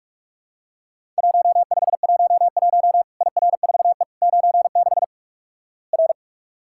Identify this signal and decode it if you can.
Morse code, heard as a keyed tone, about 44 wpm, 700 Hz. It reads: 1511 IR4E 96 R
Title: Worked All Europe DX Contest - QTC Exchange - 45wpm